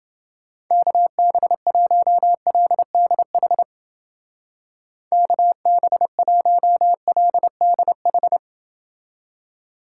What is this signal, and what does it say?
Morse code, heard as a keyed tone, about 29 wpm, 700 Hz. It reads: KB1LDH KB1LDH